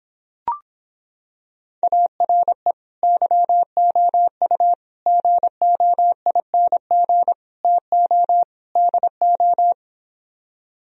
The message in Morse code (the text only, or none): ARE YOU GOING TO DO